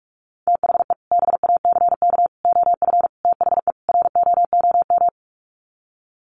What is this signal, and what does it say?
Morse code, heard as a keyed tone, about 44 wpm, 700 Hz. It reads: THE BACK OF THE ROOM
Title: Sets of 5 Words Encore 45wpm